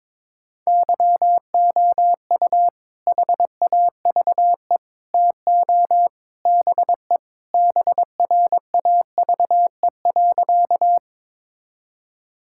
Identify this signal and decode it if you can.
Morse code, heard as a keyed tone, about 21 wpm, 700 Hz. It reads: YOU HAVE TO BE BRAVE.
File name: practice / 22wpm / Sentences with 3-Word Phrases - 22wpm